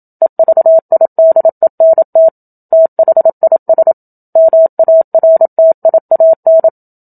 Morse code, heard as a keyed tone, about 27 wpm, 650 Hz. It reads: EVIDENT THIS MARTIAN
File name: Book Two / 01-Chapter 1 - 27 WPM